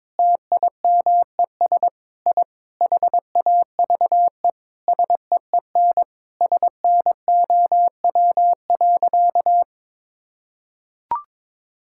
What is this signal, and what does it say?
Morse code, heard as a keyed tone, about 21 wpm, 700 Hz. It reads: TIMES I HAVE SEEN SNOW.
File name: practice / 22wpm / Sentences with 8-Word Phrases - 22wpm